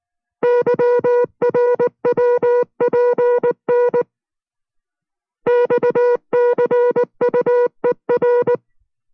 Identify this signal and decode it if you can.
Morse code, heard as a keyed tone, about 19 wpm, 490 Hz. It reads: YRWPN XCUER